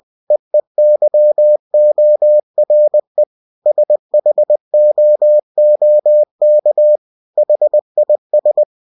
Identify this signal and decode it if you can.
Morse code, heard as a keyed tone, about 20 wpm, 600 Hz. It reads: EEYORE SHOOK HIS
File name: cw-books / winnie-the-pooh / winnie-20wpm-page-0105